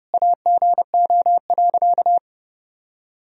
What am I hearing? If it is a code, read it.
Morse code, heard as a keyed tone, about 30 wpm, 700 Hz. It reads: AGO.
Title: Sentences from Top 500 Words and Abbreviations 30wpm